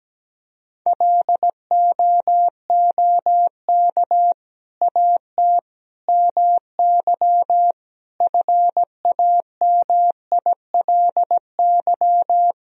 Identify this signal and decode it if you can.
Morse code, heard as a keyed tone, about 17 wpm, 700 Hz. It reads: LOOK AT MY FAMILY
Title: Sentences from Top 1000 Words - Review Part 1 - 17wpm